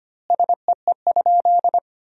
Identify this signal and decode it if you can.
Morse code, heard as a keyed tone, about 25 wpm, 700 Hz. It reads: SEE?
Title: 3x Word-Spacing - Sentences from Top 100 Words - 25wpm